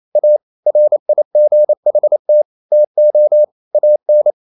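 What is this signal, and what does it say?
Morse code, heard as a keyed tone, about 28 wpm, 600 Hz. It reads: A RIGHT TO AN
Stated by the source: Morse code practice at 28 wpm, text of Ken Follett's novel Never in English, from CW med Gnister.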